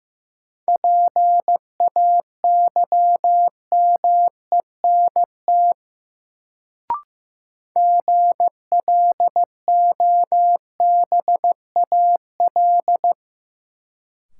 Morse code, heard as a keyed tone, about 15 wpm, 700 Hz. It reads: PAYMENT GLOBAL
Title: Sentences from Top 1200 Words - New Word Drill - 15wpm